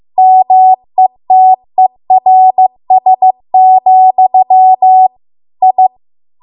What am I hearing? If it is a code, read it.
Morse code, heard as a keyed tone, about 15 wpm, 750 Hz. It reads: METERS, I